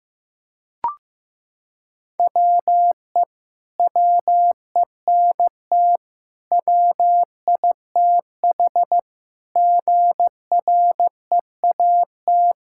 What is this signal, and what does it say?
Morse code, heard as a keyed tone, about 16 wpm, 700 Hz. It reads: WE WENT WITH GREAT
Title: Sentences from Top 1000 Words - Review Part 5 - 15wpm